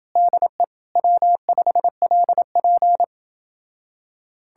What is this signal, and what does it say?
Morse code, heard as a keyed tone, about 32 wpm, 700 Hz. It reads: DE W5LP